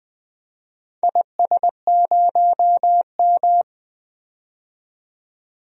Morse code, heard as a keyed tone, about 21 wpm, 700 Hz. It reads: IS0M
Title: DXCC-100 Call Signs - 4 Characters - 20wpm